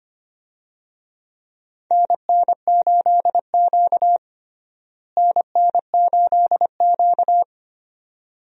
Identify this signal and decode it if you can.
Morse code, heard as a keyed tone, about 25 wpm, 700 Hz. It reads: NN8Q NN8Q
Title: USA Call Signs - 4 Characters - Repeated 2x - 25wpm